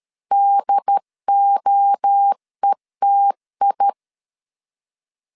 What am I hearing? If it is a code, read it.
Morse code, heard as a keyed tone, about 13 wpm, 800 Hz. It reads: DOETI
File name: CW / DET / 1 3 1500C S A 12wpm 2905